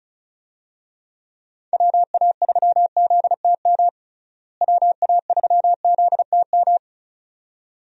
Morse code, heard as a keyed tone, about 36 wpm, 700 Hz. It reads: WA3ZTM WA3ZTM
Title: USA Call Signs - 6 Characters - Repeated 2x - 35wpm